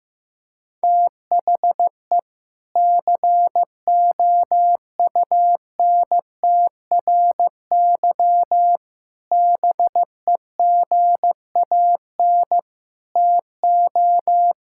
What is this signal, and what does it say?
Morse code, heard as a keyed tone, about 15 wpm, 690 Hz. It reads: THE COUNTRY BEGAN TO
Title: Sentences from Top 1000 Words - Review Part 5 - 15wpm